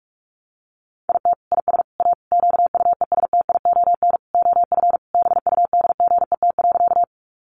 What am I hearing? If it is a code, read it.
Morse code, heard as a keyed tone, about 46 wpm, 700 Hz. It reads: IT IS A QUESTION OF BUDGET.
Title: Sentences from Top 1200 Words 45wpm